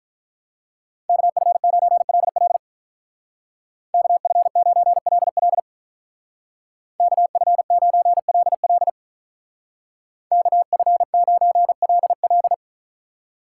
Morse code, heard as a keyed tone, about 46 wpm, 700 Hz. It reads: KF9LL KF9LL KF9LL KF9LL